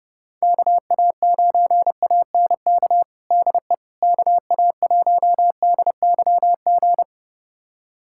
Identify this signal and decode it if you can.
Morse code, heard as a keyed tone, about 30 wpm, 700 Hz. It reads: KA9ANK DE KA1DYG
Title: USA Callsign DE Callsign - 6 Characters - 30wpm